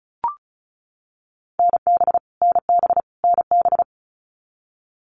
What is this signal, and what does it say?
Morse code, heard as a keyed tone, about 34 wpm, 700 Hz. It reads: NB NB NB
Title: Canadian Provinces and Territories - Repeated 3x - 35wpm